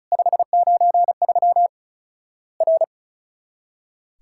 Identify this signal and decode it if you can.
Morse code, heard as a keyed tone, about 36 wpm, 700 Hz. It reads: 593 R
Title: Worked All Europe DX Contest - QTC Exchange - 35wpm